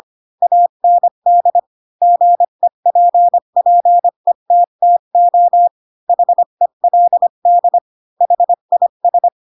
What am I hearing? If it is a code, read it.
Morse code, heard as a keyed tone, about 25 wpm, 700 Hz. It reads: AND GEPPETTO HELD HIS
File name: Pinocchio / Pinocchio ch3 20 25 ik1hge